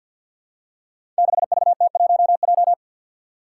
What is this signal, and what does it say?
Morse code, heard as a keyed tone, about 52 wpm, 700 Hz. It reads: 6VT1J
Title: Five Challenging Letter-Number Combinations 50wpm